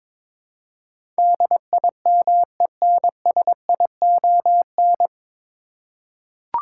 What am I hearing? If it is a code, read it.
Morse code, heard as a keyed tone, about 22 wpm, 700 Hz. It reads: DIMENSION
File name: 9 Letter Words - First 3 Letters Given - 22wpm